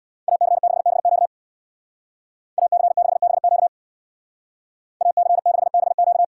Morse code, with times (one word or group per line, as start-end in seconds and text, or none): 0.3-1.3 AC6BQ
2.6-3.7 AC6BQ
5.0-6.2 AC6BQ